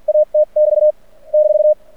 Morse code, heard as a keyed tone, about 46 wpm, spaced as for 39 wpm, 610 Hz. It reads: AT <BT> -
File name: archived / CRSnet